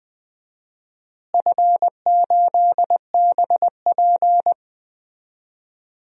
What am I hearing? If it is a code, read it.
Morse code, heard as a keyed tone, about 20 wpm, 700 Hz. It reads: F8BP
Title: DXCC-100 Call Signs - 4 Characters - Repeated 2x - 20wpm